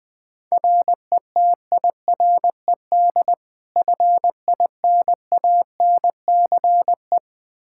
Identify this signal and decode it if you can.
Morse code, heard as a keyed tone, about 20 wpm, 700 Hz. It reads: RETIRED FINANCE
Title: Practice QSO - Exchange 3 - More Info - 20wpm